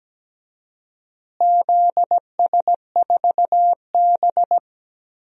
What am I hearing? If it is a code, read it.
Morse code, heard as a keyed tone, about 17 wpm, 700 Hz. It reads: ZS4B
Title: DXCC-100 Call Signs - 4 Characters - Repeated 2x - 17wpm